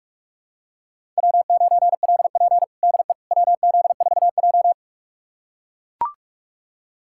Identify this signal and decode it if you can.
Morse code, heard as a keyed tone, about 43 wpm, 700 Hz. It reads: W9LP DE WZ4J
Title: USA Callsign DE Callsign - 4 Characters - 45wpm